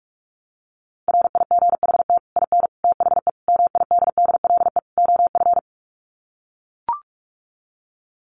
Morse code, heard as a keyed tone, about 44 wpm, 700 Hz. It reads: RIGHT IN THE MIDDLE OF
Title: Sets of 5 Words 45wpm